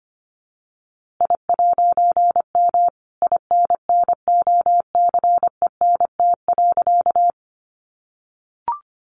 Morse code, heard as a keyed tone, about 25 wpm, 700 Hz. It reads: I'M INNOCENT.
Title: A Sound Of Thunder 25wpm